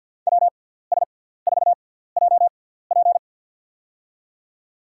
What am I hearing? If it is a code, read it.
Morse code, heard as a keyed tone, about 50 wpm, spaced as for 17 wpm, 700 Hz. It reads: WS4JP
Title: USA Call Signs - 5 Characters - 6x Character Spacing - 50wpm